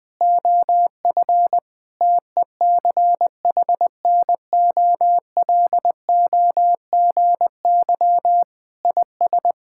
Morse code, heard as a keyed tone, about 20 wpm, 700 Hz. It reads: OF TECHNOLOGY IS
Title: Sentences from Top 1000 Words - Review Part 5 - 20wpm